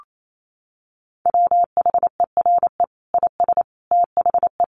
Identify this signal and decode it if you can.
Morse code, heard as a keyed tone, about 28 wpm, 700 Hz. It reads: WHERE IS THE
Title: Sentences from Top 1300 Words 28wpm